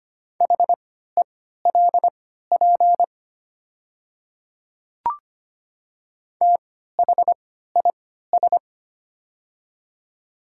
Morse code, heard as a keyed tone, about 25 wpm, spaced as for 14 wpm, 700 Hz. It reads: HELP THIS
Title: 3x Character Spacing - Top 500 Words - 25wpm